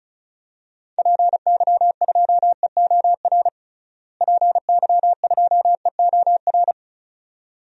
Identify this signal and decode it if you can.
Morse code, heard as a keyed tone, about 34 wpm, 700 Hz. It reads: PY2EOR PY2EOR
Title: DXCC-100 Call Signs - 6 Characters - Repeated 2x - 35wpm